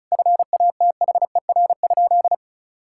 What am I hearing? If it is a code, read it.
Morse code, heard as a keyed tone, about 34 wpm, 700 Hz. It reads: FATHER?
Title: Sentences from Top 1600 Words 35wpm